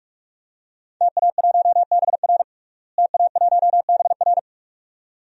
Morse code, heard as a keyed tone, about 41 wpm, 700 Hz. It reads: TA1BR TA1BR
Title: DXCC-100 Call Signs - 5 Characters - Repeated 2x - 45wpm